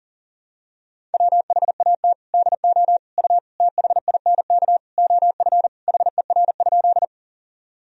Morse code, heard as a keyed tone, about 40 wpm, 700 Hz. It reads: WHAT DO U THINK OF HER?
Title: Sentences from Top 200 Words and Abbreviations 40wpm